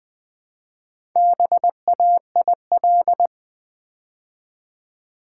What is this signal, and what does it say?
Morse code, heard as a keyed tone, about 20 wpm, 700 Hz. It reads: BAIL
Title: Build A Word - 2 Letters At A Time - 4-Letter Words - 20wpm